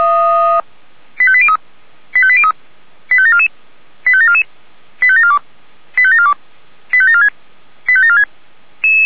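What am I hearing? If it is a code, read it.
Five-tone selective call (ZVEI), heard as quick runs of five tones, tones of about 70 ms. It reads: 85803 85803 85744 85744 85742 85742 85746 85746
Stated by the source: Received on SDR as ZVEI five-tone selective call.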